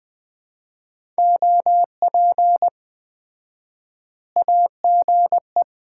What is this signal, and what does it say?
Morse code, heard as a keyed tone, about 21 wpm, 700 Hz. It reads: OP AGE
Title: Common QSO Phrases - 4x Word-Spacing - 20wpm